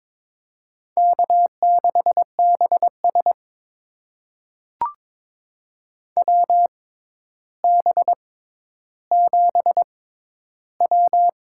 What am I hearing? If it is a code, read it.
Morse code, heard as a keyed tone, about 22 wpm, spaced as for 11 wpm, 700 Hz. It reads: K6BS WB7W
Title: USA Call Signs - 6x Character Spacing - 22wpm